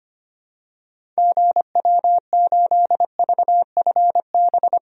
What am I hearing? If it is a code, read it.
Morse code, heard as a keyed tone, about 25 wpm, 700 Hz. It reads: GW8VFB
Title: DXCC-100 Call Signs - 6 Characters - 25wpm